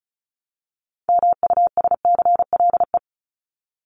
Morse code, heard as a keyed tone, about 34 wpm, 700 Hz. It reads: MUSCLE
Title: Top 1251-1500 Words 35wpm